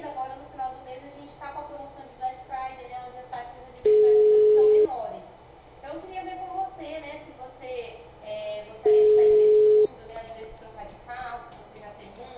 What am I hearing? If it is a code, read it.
Ringback tone (Europe, 425 Hz), with speech.